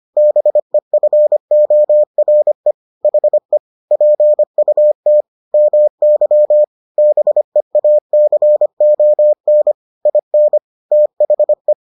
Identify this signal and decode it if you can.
Morse code, heard as a keyed tone, about 25 wpm, 600 Hz. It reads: BEFORE HE PUT MY BEACON IN THE